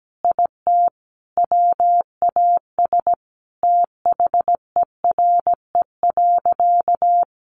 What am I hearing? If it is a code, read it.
Morse code, heard as a keyed tone, about 17 wpm, 700 Hz. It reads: IT WAS THERE.
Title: A Sound Of Thunder 17wpm